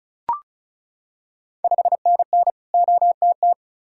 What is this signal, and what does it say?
Morse code, heard as a keyed tone, about 34 wpm, 700 Hz. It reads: 5NN OTT
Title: Ontario QSO Party Exchange 35wpm